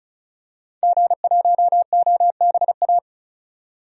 Morse code, heard as a keyed tone, about 36 wpm, 700 Hz. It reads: G1OBA
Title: DXCC-100 Call Signs - 5 Characters - 35wpm